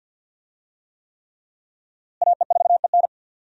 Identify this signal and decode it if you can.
Morse code, heard as a keyed tone, about 49 wpm, 700 Hz. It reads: AE4EN
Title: USA Call Signs - 5 Characters - Repeated 3x - 50wpm